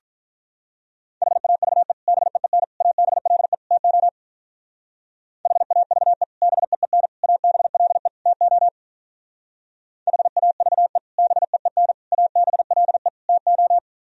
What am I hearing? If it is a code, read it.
Morse code, heard as a keyed tone, about 43 wpm, 700 Hz. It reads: HAVE BEEN ABLE TO HAVE BEEN ABLE TO HAVE BEEN ABLE TO
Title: Speed-Racing - Sets of 4 Words 35wpm